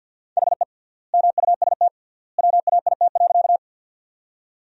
Morse code, heard as a keyed tone, about 48 wpm, 700 Hz. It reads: HE MUST WAIT.
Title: 3x Word-Spacing - Sentences from Top 500 Words - 50wpm